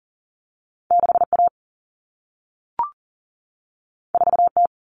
Morse code, heard as a keyed tone, about 40 wpm, 700 Hz. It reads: 6A 4T
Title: Common Cut Numbers - 2-Digit 40wpm